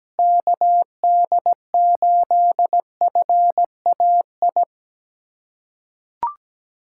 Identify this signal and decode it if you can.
Morse code, heard as a keyed tone, about 17 wpm, 700 Hz. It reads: KD8FAI
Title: USA Callsign DE Callsign 17wpm